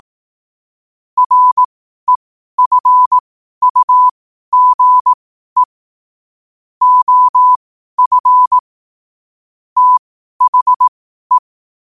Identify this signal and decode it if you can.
Morse code, heard as a keyed tone, about 18 wpm, spaced as for 12 wpm, 1000 Hz. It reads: REFUGE OF THE
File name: qotdfiles / 12wpm